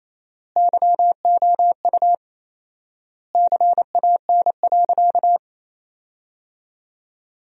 Morse code, heard as a keyed tone, about 28 wpm, 700 Hz. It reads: YOU CAN.
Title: 4x Word-Spacing - Sentences from Top 500 Words - 28wpm